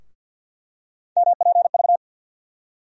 Morse code, heard as a keyed tone, about 51 wpm, 700 Hz. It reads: MPV
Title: USA Call Sign Postfixes - 3 Letters 50wpm